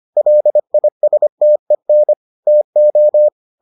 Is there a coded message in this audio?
Yes: Morse code from 0.2 to 3.3 s, heard as a keyed tone, about 25 wpm, 600 Hz. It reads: LISTEN TO